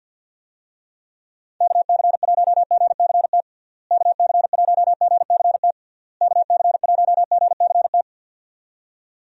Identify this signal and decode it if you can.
Morse code, heard as a keyed tone, about 52 wpm, 700 Hz. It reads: KC1GCT KC1GCT KC1GCT
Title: USA Call Signs - 6 Characters - Repeated 3x - 50wpm